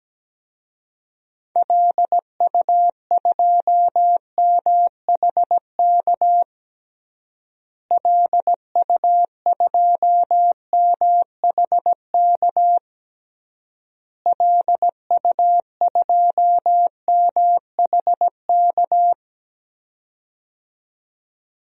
Morse code, heard as a keyed tone, about 17 wpm, 700 Hz. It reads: LU2MHK LU2MHK LU2MHK
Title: DXCC-100 Call Signs - 6 Characters - Repeated 3x - 17wpm